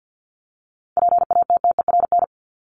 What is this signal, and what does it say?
Morse code, heard as a keyed tone, about 50 wpm, 700 Hz. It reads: PATTERN
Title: Top 500 Words 50wpm